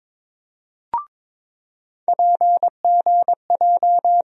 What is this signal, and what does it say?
Morse code, heard as a keyed tone, about 22 wpm, 700 Hz. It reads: PGJ
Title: USA Call Sign Postfixes - 3 Letters 22wpm